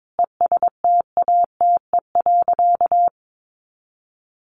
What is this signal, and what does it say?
Morse code, heard as a keyed tone, about 22 wpm, 700 Hz. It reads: ESTATE.
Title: Sentences from Top 1200 Words 22wpm